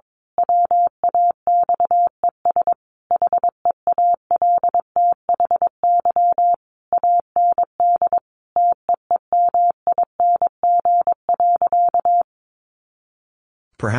Morse code, heard as a keyed tone, about 22 wpm, 700 Hz. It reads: WAXES HEALTHY AND TEEMING.
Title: A Sound Of Thunder 22wpm